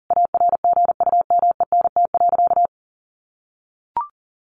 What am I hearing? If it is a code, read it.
Morse code, heard as a keyed tone, about 37 wpm, 700 Hz. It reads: ARGUMENT.
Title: Sentences from Top 1400 Words 40wpm